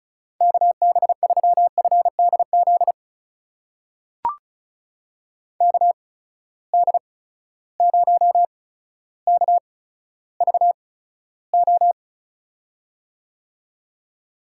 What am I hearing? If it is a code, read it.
Morse code, heard as a keyed tone, about 36 wpm, spaced as for 15 wpm, 700 Hz. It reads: KB3FDZ KD0KVO
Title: USA Call Signs - 6 Characters - 8x Character Spacing - 35wpm